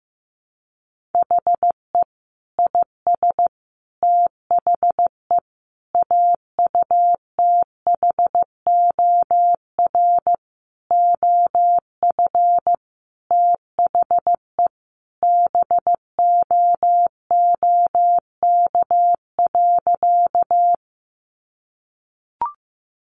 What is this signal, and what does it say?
Morse code, heard as a keyed tone, about 15 wpm, 700 Hz. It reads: HE IS THE AUTHOR OF THE BOOK.
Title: Sentences from Top 1200 Words 15wpm